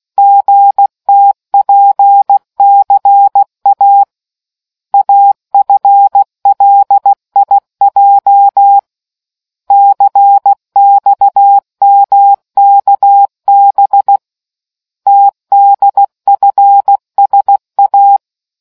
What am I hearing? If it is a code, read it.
Morse code, heard as a keyed tone, about 16 wpm, 800 Hz. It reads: GTPCA AFLIJ CXMKB TDFSA